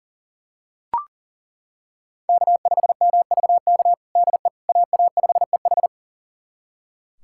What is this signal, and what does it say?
Morse code, heard as a keyed tone, about 37 wpm, 700 Hz. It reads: K5MVK DE AA5EH
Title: USA Callsign DE Callsign - 5 Characters - 40wpm